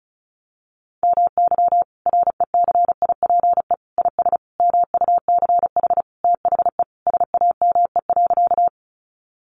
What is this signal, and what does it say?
Morse code, heard as a keyed tone, about 34 wpm, 700 Hz. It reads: MY RECIPE IS MUCH THE SAME.
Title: Sentences from Top 1200 Words 35wpm